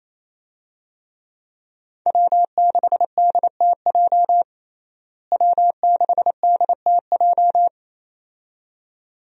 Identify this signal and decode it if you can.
Morse code, heard as a keyed tone, about 28 wpm, 700 Hz. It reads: W6DTJ W6DTJ
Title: USA Call Signs - 5 Characters - Repeated 2x - 28wpm